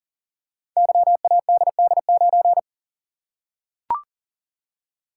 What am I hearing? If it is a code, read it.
Morse code, heard as a keyed tone, about 39 wpm, 700 Hz. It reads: YADD9
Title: Code Groups - 5 Letter:Numbers - 40wpm